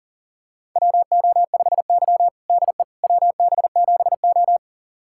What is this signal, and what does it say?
Morse code, heard as a keyed tone, about 40 wpm, 700 Hz. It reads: WO5Y DE WB7O
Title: USA Callsign DE Callsign - 4 Characters - 40wpm